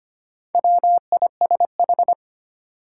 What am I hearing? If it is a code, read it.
Morse code, heard as a keyed tone, about 25 wpm, 700 Hz. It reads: WISH
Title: Sentences from Top 1600 Words - New Word Drill Alt - 25wpm